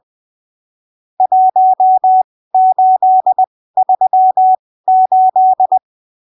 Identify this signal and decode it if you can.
Morse code, heard as a keyed tone, about 20 wpm, spaced as for 15 wpm, 750 Hz. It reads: 1838
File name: arquivos / texto / texto 17 20ppm